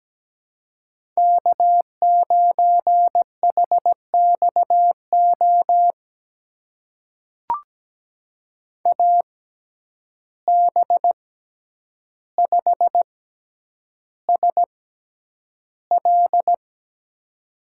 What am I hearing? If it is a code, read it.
Morse code, heard as a keyed tone, about 17 wpm, spaced as for 9 wpm, 700 Hz. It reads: K9HXO AB5SL
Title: USA Call Signs - 5 Characters - 6x Character Spacing - 17wpm